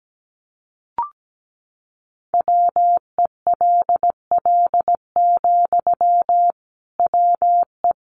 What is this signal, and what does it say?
Morse code, heard as a keyed tone, about 17 wpm, 700 Hz. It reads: WELL, WE
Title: Sentences from Top 1100 Words 17wpm